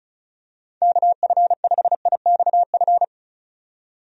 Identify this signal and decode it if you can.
Morse code, heard as a keyed tone, about 34 wpm, 700 Hz. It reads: KF5IXF
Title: USA Call Signs - 6 Characters - 35wpm